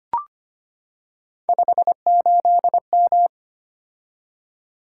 Morse code, heard as a keyed tone, about 25 wpm, 700 Hz. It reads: 58M
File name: Three Letter-Number Combinations - 25wpm